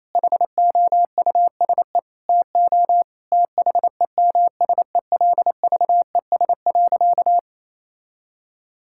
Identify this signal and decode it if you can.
Morse code, heard as a keyed tone, about 28 wpm, 700 Hz. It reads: HOUSE TO THEMSELVES.
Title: Sentences from Top 1000 Words - Review Part 5 - 28wpm